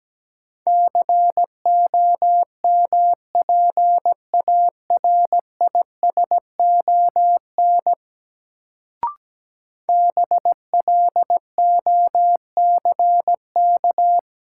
Morse code, heard as a keyed tone, about 17 wpm, 700 Hz. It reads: COMPARISON BLOCK
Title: Sentences from Top 1600 Words - New Word Drill Alt - 17wpm